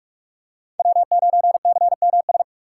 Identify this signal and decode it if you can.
Morse code, heard as a keyed tone, about 47 wpm, 700 Hz. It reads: W9CMS
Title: USA Call Signs - 5 Characters - 2x Character Spacing - 45wpm